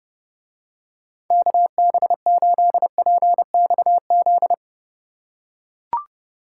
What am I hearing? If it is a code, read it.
Morse code, heard as a keyed tone, about 29 wpm, 700 Hz. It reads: KB8PXZ